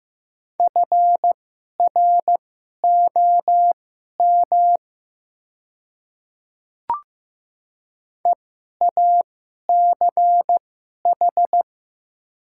Morse code, heard as a keyed tone, about 15 wpm, spaced as for 11 wpm, 700 Hz. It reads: FROM EACH
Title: 2x Character Spacing - Top 100 Words - 15wpm